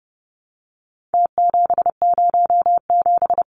Morse code, heard as a keyed tone, about 30 wpm, 700 Hz. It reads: T707
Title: Numbers - Contest Style - 4-Digit 30wpm